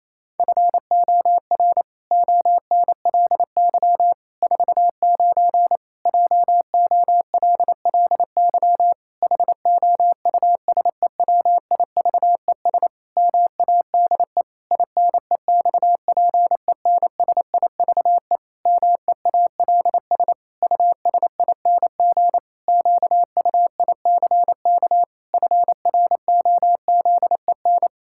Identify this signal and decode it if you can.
Morse code, heard as a keyed tone, about 28 wpm, 700 Hz. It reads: FOR ONLY 49 JOLLY HOUSEWIVES MADE INEXPENSIVE MEALS USING QUICK FROZEN
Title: Pangram Challenge v2 - 28wpm